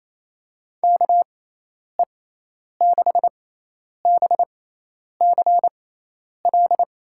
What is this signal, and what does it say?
Morse code, heard as a keyed tone, about 27 wpm, spaced as for 10 wpm, 700 Hz. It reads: KE6BCL
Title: USA Call Signs - 6x Character Spacing - 28wpm